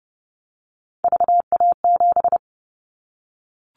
Morse code, heard as a keyed tone, about 32 wpm, 700 Hz. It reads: VA7
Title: Callsign Prefixes for Top 50 DXCC - 30wpm